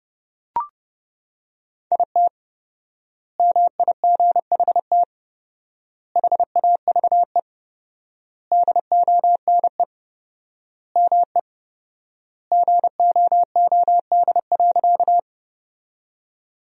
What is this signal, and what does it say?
Morse code, heard as a keyed tone, about 30 wpm, 700 Hz. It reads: IT MIGHT HAVE DONE ME GOOD.
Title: 4x Word-Spacing - Sentences from Top 500 Words - 30wpm